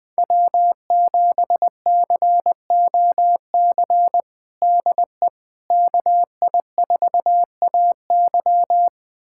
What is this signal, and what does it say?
Morse code, heard as a keyed tone, about 21 wpm, 700 Hz. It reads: W7COC DE KI4AY
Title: USA Callsign DE Callsign - 5 Characters - 20wpm